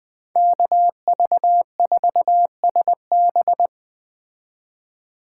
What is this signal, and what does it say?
Morse code, heard as a keyed tone, about 20 wpm, 700 Hz. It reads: KV4SB
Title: Speed Racing - USA 2x2 Call Signs 17wpm